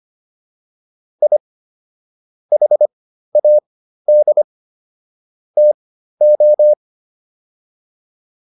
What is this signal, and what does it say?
Morse code, heard as a keyed tone, about 25 wpm, spaced as for 13 wpm, 600 Hz. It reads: I HAD TO